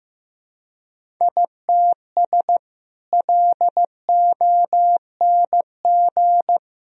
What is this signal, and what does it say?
Morse code, heard as a keyed tone, about 15 wpm, 700 Hz. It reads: ITS LONG
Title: Sentences from Top 1000 Words - Review Part 1 - 15wpm